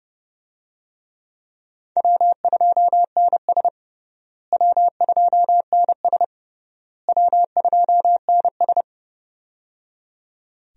Morse code, heard as a keyed tone, about 29 wpm, 700 Hz. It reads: W2NS W2NS W2NS
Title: USA Call Signs - 4 Characters - Repeated 3x - 30wpm